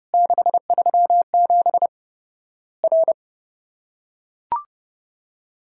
Morse code, heard as a keyed tone, about 30 wpm, 700 Hz. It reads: 637 R
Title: Worked All Europe DX Contest - QTC Exchange - 30wpm